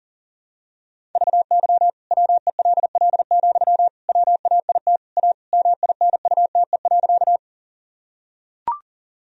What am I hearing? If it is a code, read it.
Morse code, heard as a keyed tone, about 40 wpm, 700 Hz. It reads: VY WELL, WAIT A MINUTE.
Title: Sentences from Top 500 Words and Abbreviations 40wpm